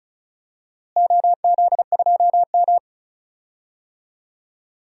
Morse code, heard as a keyed tone, about 36 wpm, 700 Hz. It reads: OZ2M
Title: DXCC-100 Call Signs - 4 Characters - Repeated 2x - 35wpm